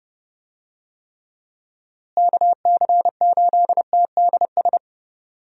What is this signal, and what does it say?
Morse code, heard as a keyed tone, about 32 wpm, 700 Hz. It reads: KC8TDS
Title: USA Call Signs - 6 Characters - Repeated 3x - 30wpm